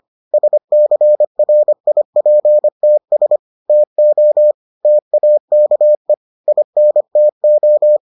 Morse code, heard as a keyed tone, about 25 wpm, 600 Hz. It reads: SCRIPTS TO TAKE INTO